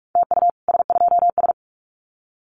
Morse code, heard as a keyed tone, about 44 wpm, 700 Hz. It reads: TU S2S
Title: Common QSO Phrases Encore 45wpm